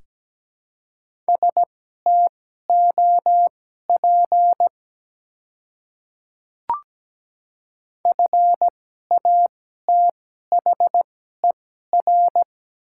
Morse code, heard as a keyed tone, about 17 wpm, spaced as for 12 wpm, 700 Hz. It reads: STOP FATHER
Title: 2x Character Spacing - Top 300 Words - 17wpm